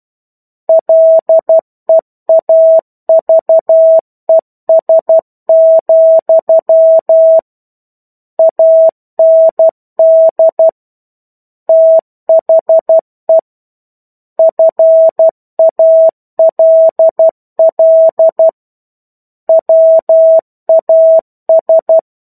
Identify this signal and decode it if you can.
Morse code, heard as a keyed tone, about 12 wpm, 650 Hz. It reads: LEAVES, AND THE FALL WAS
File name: 01-Chapter-01 12 WPM